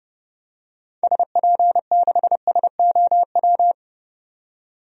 Morse code, heard as a keyed tone, about 32 wpm, 700 Hz. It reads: SP6SOW
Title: DXCC-100 Call Signs - 6 Characters - 30wpm